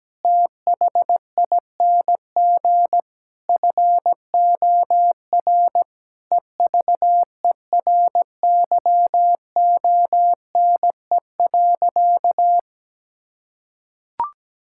Morse code, heard as a keyed tone, about 17 wpm, 700 Hz. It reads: THING FOR EVERYONE.
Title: Sentences from Top 1000 Words - Review Part 5 - 17wpm